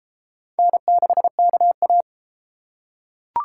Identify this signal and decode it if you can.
Morse code, heard as a keyed tone, about 32 wpm, 700 Hz. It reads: N6KA